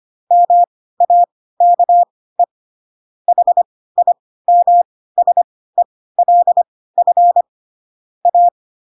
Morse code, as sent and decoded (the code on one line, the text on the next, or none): -- .- -.- . / .... .. -- ... .-.. ..-. / .-
MAKE HIMSELF A